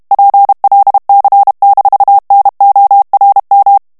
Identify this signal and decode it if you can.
Morse code, heard as a keyed tone, about 32 wpm, 800 Hz. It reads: PLC-NORM
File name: ALLEMAND texte clair 1 vitesse 30 mots minute